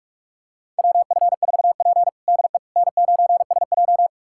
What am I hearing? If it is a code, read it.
Morse code, heard as a keyed tone, about 42 wpm, 700 Hz. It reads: WF4P DE N9SJ
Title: USA Callsign DE Callsign - 4 Characters - 45wpm